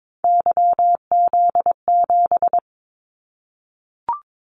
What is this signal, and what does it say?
Morse code, heard as a keyed tone, about 22 wpm, 700 Hz. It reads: YZ7
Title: Callsign Prefixes for Top 25 DXCC - 22wpm